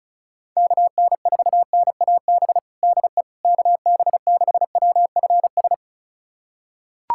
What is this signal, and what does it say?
Morse code, heard as a keyed tone, about 36 wpm, 700 Hz. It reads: KN4NAB DE KB6WFS
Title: USA Callsign DE Callsign - 6 Characters - 35wpm